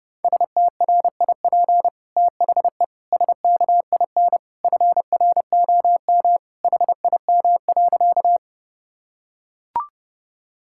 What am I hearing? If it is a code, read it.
Morse code, heard as a keyed tone, about 30 wpm, 700 Hz. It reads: STRIP THE SKIN FROM HIM.
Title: Sentences from Top 1600 Words 30wpm